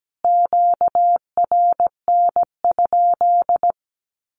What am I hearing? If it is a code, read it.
Morse code, heard as a keyed tone, about 17 wpm, 700 Hz. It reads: QRN?
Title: Q-Codes Repeated 3x 17wpm